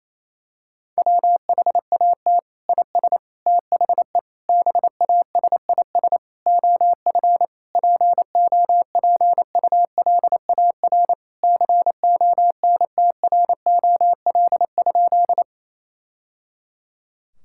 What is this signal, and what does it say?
Morse code, heard as a keyed tone, about 28 wpm, 700 Hz. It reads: WHAT IS THE BASIS OF POPULAR CONTROL?
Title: Sentences from Top 1000 Words - Review Part 5 - 28wpm